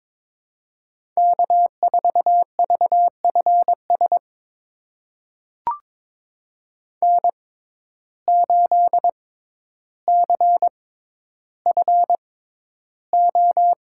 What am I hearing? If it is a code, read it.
Morse code, heard as a keyed tone, about 22 wpm, spaced as for 11 wpm, 700 Hz. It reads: K4VFS N8CFO
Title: USA Call Signs - 5 Characters - 6x Character Spacing - 22wpm